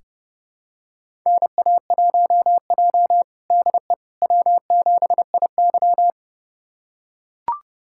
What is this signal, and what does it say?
Morse code, heard as a keyed tone, about 32 wpm, 700 Hz. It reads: NA1J DE W7IY
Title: USA Callsign DE Callsign - 4 Characters - 30wpm